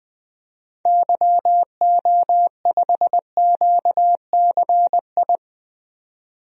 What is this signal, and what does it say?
Morse code, heard as a keyed tone, about 20 wpm, 700 Hz. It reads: YO5QCI
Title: DXCC-100 Call Signs - 6 Characters - 20wpm